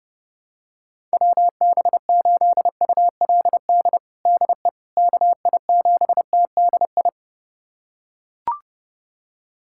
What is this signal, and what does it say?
Morse code, heard as a keyed tone, about 30 wpm, 700 Hz. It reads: WB8ULD DE KI7TDI
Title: USA Callsign DE Callsign - 6 Characters - 30wpm